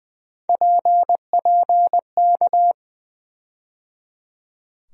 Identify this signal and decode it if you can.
Morse code, heard as a keyed tone, about 20 wpm, 700 Hz. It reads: PPK
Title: USA Call Sign Postfixes - 3 Letters 20wpm